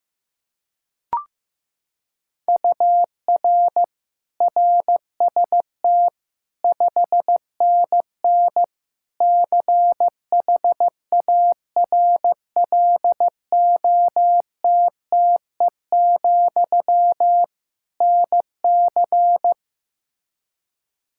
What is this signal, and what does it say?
Morse code, heard as a keyed tone, about 15 wpm, 700 Hz. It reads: UR RST 5NN CHARLOTTE, NC
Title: K1USN SST QSO Practice 15wpm